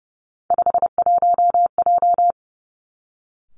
Morse code, heard as a keyed tone, about 31 wpm, 700 Hz. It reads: 51J
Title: Three Challenging Letter-Number Combinations 30wpm